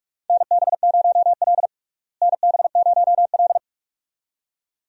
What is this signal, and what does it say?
Morse code, heard as a keyed tone, about 47 wpm, 700 Hz. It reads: NB0L NB0L
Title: USA Call Signs - 4 Characters - Repeated 2x - 45wpm